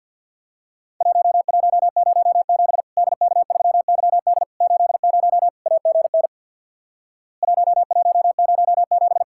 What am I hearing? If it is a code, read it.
Morse code, heard as a keyed tone, about 50 wpm, 700 Hz. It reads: 1107 DK3YD 80 AGN 1107
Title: Worked All Europe DX Contest - QTC Exchange with Repeats - 50wpm